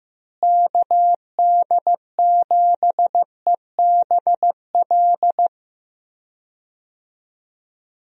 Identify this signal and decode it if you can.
Morse code, heard as a keyed tone, about 16 wpm, 700 Hz. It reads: KD7EBL
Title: USA Call Signs - 6 Characters - Repeated 3x - 15wpm